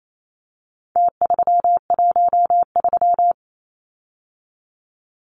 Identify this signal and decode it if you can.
Morse code, heard as a keyed tone, about 28 wpm, 700 Hz. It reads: T313
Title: Numbers - Contest Style - 4-Digit 28wpm